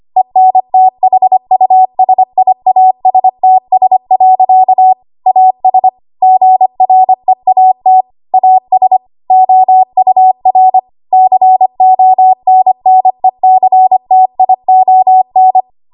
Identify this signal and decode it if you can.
Morse code, heard as a keyed tone, about 25 wpm, 750 Hz. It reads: ENTHUSIASTS. AS GREAT AS OUR CONNECTION
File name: Archive / 25 WPM / 25WPM